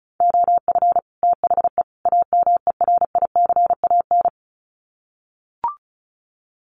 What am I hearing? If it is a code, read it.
Morse code, heard as a keyed tone, about 34 wpm, 700 Hz. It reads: OF THE AMERICAN
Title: Sets of 3 Words 35wpm